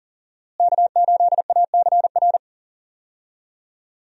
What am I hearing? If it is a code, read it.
Morse code, heard as a keyed tone, about 40 wpm, 700 Hz. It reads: K8ACR
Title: USA Call Signs - 5 Characters - Repeated 2x - 40wpm